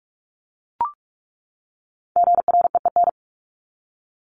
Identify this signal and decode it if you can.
Morse code, heard as a keyed tone, about 44 wpm, 700 Hz. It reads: GREEN
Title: Top 500 Words 45wpm